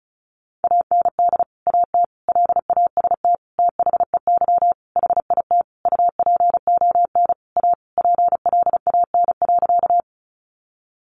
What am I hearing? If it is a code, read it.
Morse code, heard as a keyed tone, about 34 wpm, 700 Hz. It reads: AND AT LAST THEY HIT UPON A PLAN.
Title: Sentences from Top 1200 Words 35wpm